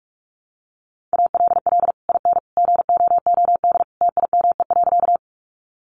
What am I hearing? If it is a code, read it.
Morse code, heard as a keyed tone, about 44 wpm, 700 Hz. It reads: ALL IN GOOD TIME.
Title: Sentences from Top 200 Words 45wpm